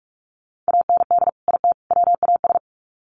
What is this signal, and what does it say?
Morse code, heard as a keyed tone, about 44 wpm, 700 Hz. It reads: AND IT WAS
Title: Sets of 3 Words 45wpm